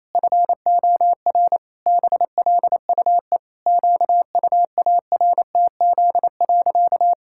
Morse code, heard as a keyed tone, about 28 wpm, 700 Hz. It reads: FOR BLUE QUARTZ.
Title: Pangram Challenge v2 - 28wpm